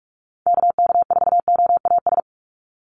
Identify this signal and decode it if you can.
Morse code, heard as a keyed tone, about 44 wpm, 700 Hz. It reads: KK4OAS
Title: USA Call Signs 45wpm